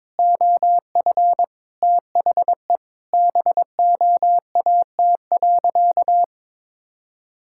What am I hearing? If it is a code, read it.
Morse code, heard as a keyed tone, about 22 wpm, 700 Hz. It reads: OF THE BOAT.
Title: Sentences from Top 1000 Words - Review Part 6 - 22wpm